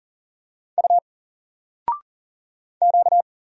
Morse code, heard as a keyed tone, about 40 wpm, 700 Hz. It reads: U Q
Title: Single Letter-Number 40wpm